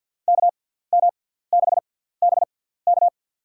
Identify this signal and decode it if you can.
Morse code, heard as a keyed tone, about 46 wpm, spaced as for 17 wpm, 690 Hz. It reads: KM6BK
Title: USA Call Signs - 5 Characters - 6x Character Spacing - 50wpm